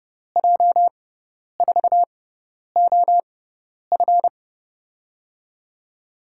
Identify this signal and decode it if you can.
Morse code, heard as a keyed tone, about 30 wpm, spaced as for 10 wpm, 700 Hz. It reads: J4OF
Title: USA Call Signs - 5 Characters - 6x Character Spacing - 30wpm